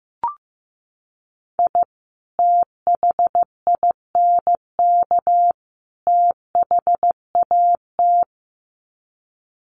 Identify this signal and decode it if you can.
Morse code, heard as a keyed tone, about 15 wpm, 700 Hz. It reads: I THINK THAT
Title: Sets of 3 Words 15wpm